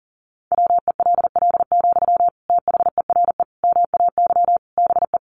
Morse code, heard as a keyed tone, about 40 wpm, 700 Hz. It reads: WELL, THERE MAY BE
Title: Sentences from Top 1300 Words 40wpm